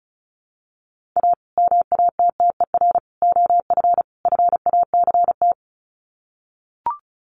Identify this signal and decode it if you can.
Morse code, heard as a keyed tone, about 34 wpm, 700 Hz. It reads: A MATTER OF FACT
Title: Sets of 4 Words 35wpm